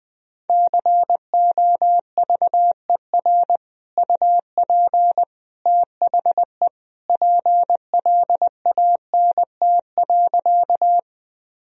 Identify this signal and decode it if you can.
Morse code, heard as a keyed tone, about 20 wpm, 700 Hz. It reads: COVER UP THE PLANT.
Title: Sentences from Top 1000 Words - Review Part 1 - 20wpm